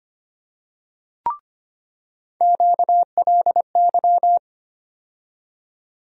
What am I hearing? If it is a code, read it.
Morse code, heard as a keyed tone, about 25 wpm, 700 Hz. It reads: QLY
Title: USA Call Sign Postfixes - 3 Letters 25wpm